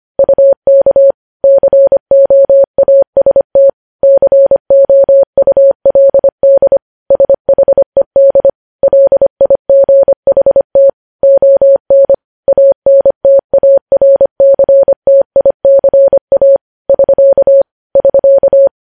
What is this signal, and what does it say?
Morse code, heard as a keyed tone, about 25 wpm, 550 Hz. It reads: UK COAST COULD SHED LIGHT ON ANTARCTICA <SK> <SK>
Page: News Headlines in Morse Code at 25 WPM